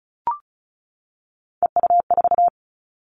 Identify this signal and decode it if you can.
Morse code, heard as a keyed tone, about 34 wpm, 700 Hz. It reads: EU4
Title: All Cut Numbers - 3-Digit 35wpm